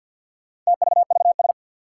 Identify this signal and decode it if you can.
Morse code, heard as a keyed tone, about 50 wpm, 700 Hz. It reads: TVVS
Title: Four Challenging Letter-Number Combinations 50wpm